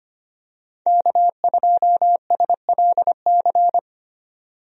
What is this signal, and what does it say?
Morse code, heard as a keyed tone, about 25 wpm, 700 Hz. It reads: K2SLC
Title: USA Call Signs - 5 Characters - 25wpm